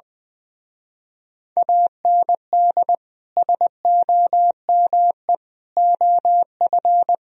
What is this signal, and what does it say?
Morse code, heard as a keyed tone, about 20 wpm, 700 Hz. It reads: AND SOME OF
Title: Speed-Racing - Sets of 4 Words 20wpm